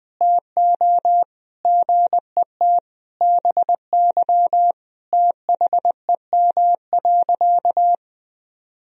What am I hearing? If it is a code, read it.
Morse code, heard as a keyed tone, about 20 wpm, 700 Hz. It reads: TO GET BY THEM.
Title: Sentences from Top 1000 Words - Review Part 1 - 20wpm